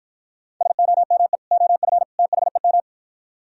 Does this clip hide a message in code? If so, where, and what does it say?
Yes: Morse code from 0.6 to 2.8 s, heard as a keyed tone, about 52 wpm, 700 Hz. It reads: SOME OF THEM